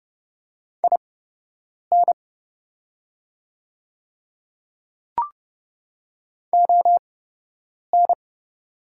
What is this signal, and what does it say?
Morse code, heard as a keyed tone, about 29 wpm, spaced as for 8 wpm, 700 Hz. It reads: IN ON
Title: 8x Character Spacing - Top 100 Words - 30wpm